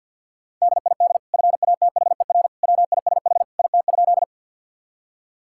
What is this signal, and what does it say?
Morse code, heard as a keyed tone, about 52 wpm, 700 Hz. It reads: DID FATHER WISH IT?
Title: Sentences from Top 1600 Words 50wpm